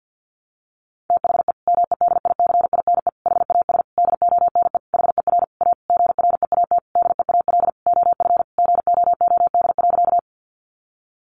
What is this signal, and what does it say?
Morse code, heard as a keyed tone, about 50 wpm, 700 Hz. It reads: THE MEDICINE HAS DONE HER A GREAT DEAL OF GOOD.
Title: Sentences from Top 900 Words 50wpm